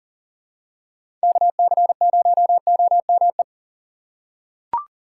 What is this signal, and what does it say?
Morse code, heard as a keyed tone, about 39 wpm, 700 Hz. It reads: KC0OME
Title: USA Call Signs - 6 Characters - 3x Character Spacing - 40wpm